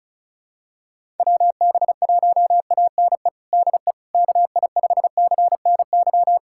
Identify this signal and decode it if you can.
Morse code, heard as a keyed tone, about 34 wpm, 700 Hz. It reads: WB1ANE DE KI5CNY
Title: USA Callsign DE Callsign - 6 Characters - 35wpm